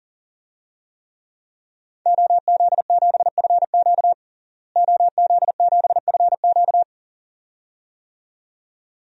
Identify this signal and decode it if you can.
Morse code, heard as a keyed tone, about 37 wpm, 700 Hz. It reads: OZ7FQ OZ7FQ
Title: DXCC-100 Call Signs - 5 Characters - Repeated 2x - 40wpm